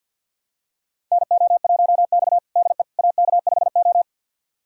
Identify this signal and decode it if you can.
Morse code, heard as a keyed tone, about 53 wpm, 700 Hz. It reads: NO1X DE AK5O
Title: USA Callsign DE Callsign - 4 Characters - 50wpm